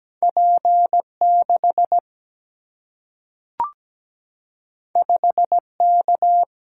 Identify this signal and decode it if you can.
Morse code, heard as a keyed tone, about 17 wpm, 700 Hz. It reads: P 6 5 K
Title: Single-Letter Single-Number 17wpm